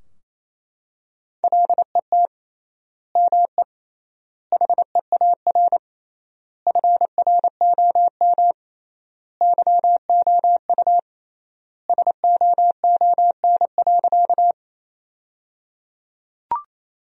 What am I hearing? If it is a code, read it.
Morse code, heard as a keyed tone, about 28 wpm, 700 Hz. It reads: LET ME HEAR FROM YOU SOON.
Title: 3x Word-Spacing - Sentences from Top 400 Words - 28wpm